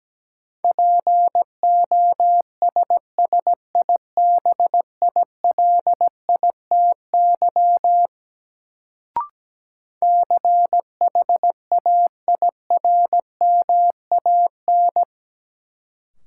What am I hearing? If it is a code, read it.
Morse code, heard as a keyed tone, about 17 wpm, 700 Hz. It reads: POSSIBILITY CHAIRMAN
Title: Sets of 3 Words - New Word Drill - 17wpm